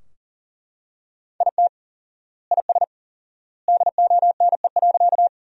Morse code, heard as a keyed tone, about 40 wpm, 700 Hz. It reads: IT IS DONE.
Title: 4x Word-Spacing - Sentences from Top 500 Words - 40wpm